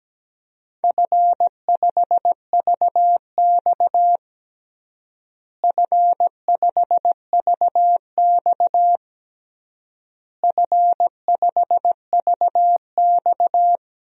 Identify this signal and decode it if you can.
Morse code, heard as a keyed tone, about 17 wpm, 700 Hz. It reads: F5VX F5VX F5VX
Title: DXCC-100 Call Signs - 4 Characters - Repeated 3x - 17wpm